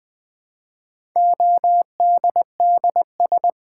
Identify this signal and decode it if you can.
Morse code, heard as a keyed tone, about 20 wpm, 700 Hz. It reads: ODDS
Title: Build A Word - 1 Letter At A Time - 4-Letter Words - 20wpm